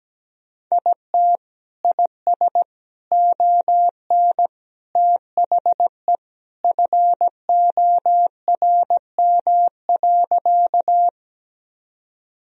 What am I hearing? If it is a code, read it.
Morse code, heard as a keyed tone, about 17 wpm, 700 Hz. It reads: IT IS ON THE FORM.
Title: Sentences from Top 1000 Words - Review Part 1 - 17wpm